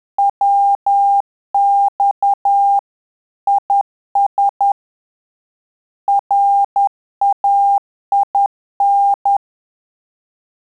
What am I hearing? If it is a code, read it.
Morse code, heard as a keyed tone, about 10 wpm, 800 Hz. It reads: WX IS RAIN
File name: Telegraphy-group / QSO CW 10 PPM